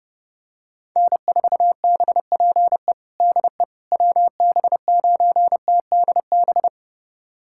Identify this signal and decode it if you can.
Morse code, heard as a keyed tone, about 32 wpm, 700 Hz. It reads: N4BPE DE WB9TDB
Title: USA Callsign DE Callsign 30wpm